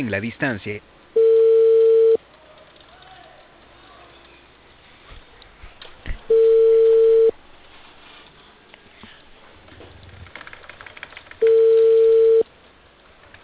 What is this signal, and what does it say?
Ringback tone (China, 450 Hz), with speech.